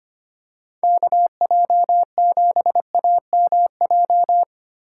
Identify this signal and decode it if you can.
Morse code, heard as a keyed tone, about 25 wpm, 700 Hz. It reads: KJ7AMJ
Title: USA Call Signs - 6 Characters - 2x Character Spacing - 25wpm